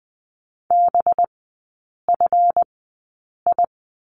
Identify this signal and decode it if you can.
Morse code, heard as a keyed tone, about 20 wpm, spaced as for 8 wpm, 700 Hz. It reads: BFI
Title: 184 Lesson 30e - Letter b - Three Characters 20wpm